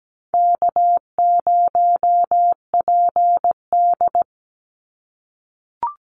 Morse code, heard as a keyed tone, about 17 wpm, 700 Hz. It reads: K0PD
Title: Easy Call Signs - 17wpm